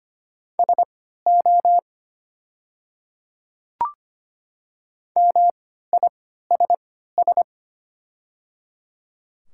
Morse code, heard as a keyed tone, about 25 wpm, spaced as for 14 wpm, 690 Hz. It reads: SO MISS
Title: 3x Character Spacing - Top 500 Words - 25wpm